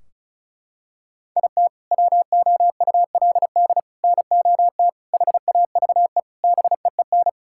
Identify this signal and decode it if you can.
Morse code, heard as a keyed tone, about 34 wpm, 700 Hz. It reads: IT WOULD NOT HAVE BEEN
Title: Sentences from Top 1600 Words 35wpm